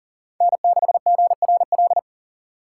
Morse code, heard as a keyed tone, about 40 wpm, 700 Hz. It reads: N6GRL
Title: USA Call Signs - 5 Characters - 3x Character Spacing - 40wpm